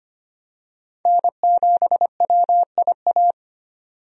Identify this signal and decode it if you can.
Morse code, heard as a keyed tone, about 25 wpm, 700 Hz. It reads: N7WIA
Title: USA Call Signs - 5 Characters - 25wpm